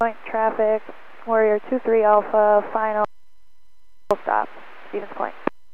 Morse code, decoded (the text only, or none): none